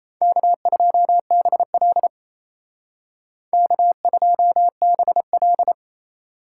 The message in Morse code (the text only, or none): K2BL K2BL